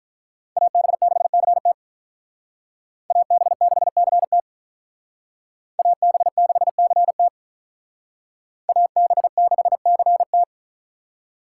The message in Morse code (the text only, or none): AB6CT AB6CT AB6CT AB6CT